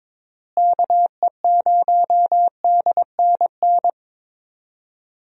Morse code, heard as a keyed tone, about 21 wpm, 700 Hz. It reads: KE0DNN
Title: USA Call Signs - 6 Characters - Repeated 2x - 22wpm